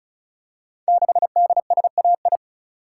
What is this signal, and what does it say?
Morse code, heard as a keyed tone, about 35 wpm, 700 Hz. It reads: 6DSAI
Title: Code Groups - 5 Letter:Numbers - 35wpm